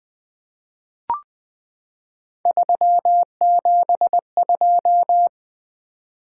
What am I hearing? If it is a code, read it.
Morse code, heard as a keyed tone, about 20 wpm, 700 Hz. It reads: 372
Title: Three Letter-Number Combinations - 20wpm